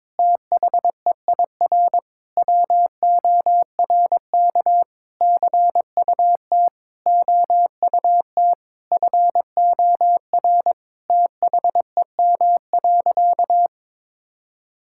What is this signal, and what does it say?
Morse code, heard as a keyed tone, about 22 wpm, 700 Hz. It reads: THEIR WORK CUT OUT FOR THEM.
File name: Sentences with 7-Word Phrases - 22wpm